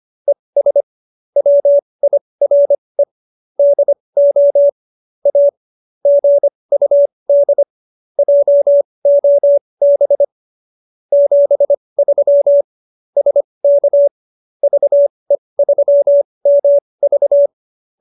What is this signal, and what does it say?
Morse code, heard as a keyed tone, about 25 wpm, spaced as for 20 wpm, 570 Hz. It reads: ES WIRE DO A GUD JOB 73 SK VE3MV